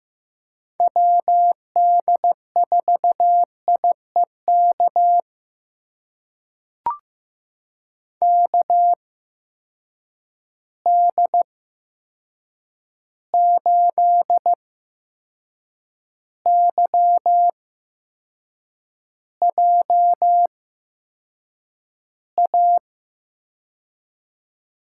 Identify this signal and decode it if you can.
Morse code, heard as a keyed tone, about 15 wpm, spaced as for 6 wpm, 700 Hz. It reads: WD4IEK KD8YJA
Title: USA Call Signs - 6 Characters - 8x Character Spacing - 15wpm